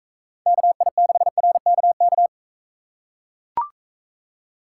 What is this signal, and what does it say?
Morse code, heard as a keyed tone, about 41 wpm, 700 Hz. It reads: KI6RKK